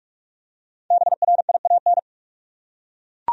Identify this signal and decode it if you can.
Morse code, heard as a keyed tone, about 42 wpm, 700 Hz. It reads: BRIAN
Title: USA Common Names - Max 5 Letters - 45wpm